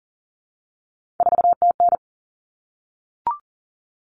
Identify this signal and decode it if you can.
Morse code, heard as a keyed tone, about 40 wpm, 700 Hz. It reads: 4TN
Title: Common Cut Numbers - 3-Digit 40wpm